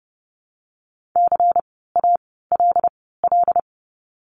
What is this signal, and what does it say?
Morse code, heard as a keyed tone, about 30 wpm, spaced as for 17 wpm, 700 Hz. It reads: CALL
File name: course / 30:10wpm / 237 Lesson 38f - Letter q - Words 30wpm